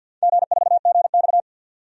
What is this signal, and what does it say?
Morse code, heard as a keyed tone, about 47 wpm, 700 Hz. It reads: G4GX
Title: DXCC-100 Call Signs - 4 Characters - 50wpm